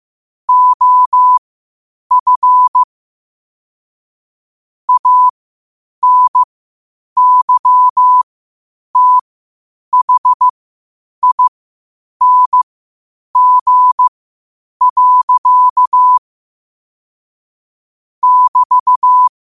Morse code, heard as a keyed tone, about 15 wpm, spaced as for 8 wpm, 1000 Hz. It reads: OF ANYTHING. <BT>
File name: qotdfiles / 8wpm